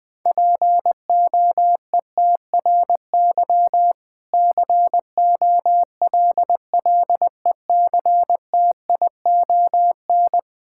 Morse code, heard as a keyed tone, about 20 wpm, 700 Hz. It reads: POETRY COLLECTION
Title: Difficult Words with Hints - 20wpm